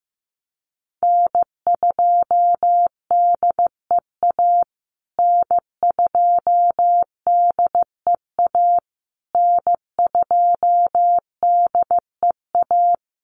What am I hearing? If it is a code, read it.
Morse code, heard as a keyed tone, about 15 wpm, 700 Hz. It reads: N2DEA N2DEA N2DEA
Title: USA Call Signs Repeated 3x 15wpm